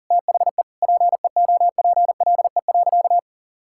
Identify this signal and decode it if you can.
Morse code, heard as a keyed tone, about 40 wpm, 700 Hz. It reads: THE PEOPLE.
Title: Sentences from Top 1000 Words - Review Part 6 - 40wpm